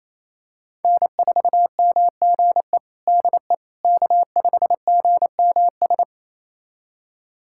Morse code, heard as a keyed tone, about 27 wpm, 700 Hz. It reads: N4MGE DE K5GMS
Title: USA Callsign DE Callsign - 5 Characters - 28wpm